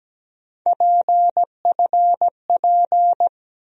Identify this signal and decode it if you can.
Morse code, heard as a keyed tone, about 17 wpm, 700 Hz. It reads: PFP
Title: USA Call Sign Postfixes - 3 Letters 17wpm